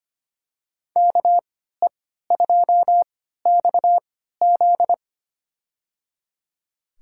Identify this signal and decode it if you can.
Morse code, heard as a keyed tone, about 25 wpm, spaced as for 14 wpm, 700 Hz. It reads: KE2XZ
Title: USA Call Signs - 5 Characters - 3x Character Spacing - 25wpm